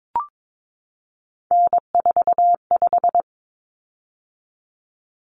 Morse code, heard as a keyed tone, about 22 wpm, 700 Hz. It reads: N45
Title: Common Cut Numbers - 3-Digit 22wpm